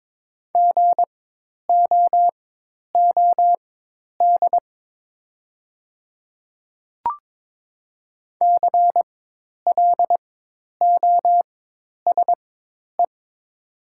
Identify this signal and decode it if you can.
Morse code, heard as a keyed tone, about 22 wpm, spaced as for 10 wpm, 700 Hz. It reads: GOOD CLOSE
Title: 4x Character Spacing - Top 500 Words - 22wpm